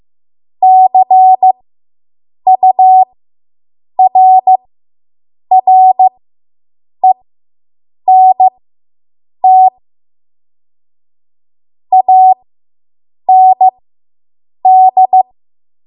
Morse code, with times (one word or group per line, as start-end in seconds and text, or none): 0.6-9.7 CURRENT
11.9-15.2 AND